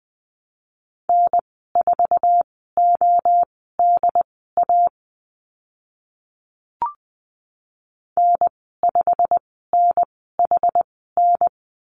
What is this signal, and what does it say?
Morse code, heard as a keyed tone, about 20 wpm, spaced as for 14 wpm, 700 Hz. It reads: N4ODA N5NHN
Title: 088 Lesson 15g - Number 5 - Call Signs 20wpm